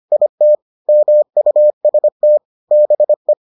Morse code, heard as a keyed tone, about 25 wpm, 600 Hz. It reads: IT MUST BE